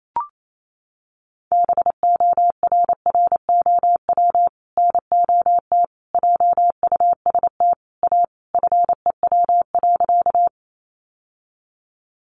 Morse code, heard as a keyed tone, about 28 wpm, 700 Hz. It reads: BORROW NOT JUST A FEW.
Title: Sentences from Top 1500 Words 28wpm